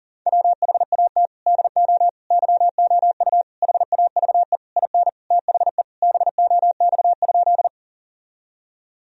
Morse code, heard as a keyed tone, about 40 wpm, 700 Hz. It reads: WHAT DO YOU HAVE IN THE BOX?
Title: Sentences from Top 1000 Words - Review Part 2 - 40wpm